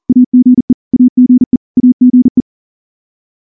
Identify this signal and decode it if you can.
Morse code, heard as a keyed tone, about 38 wpm, 250 Hz. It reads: AGE AGE AGE
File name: morse / morse-training-40WPM